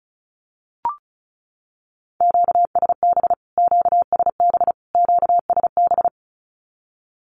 Morse code, heard as a keyed tone, about 34 wpm, 700 Hz. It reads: QSB QSB QSB
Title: Q-Codes Repeated 3x 35wpm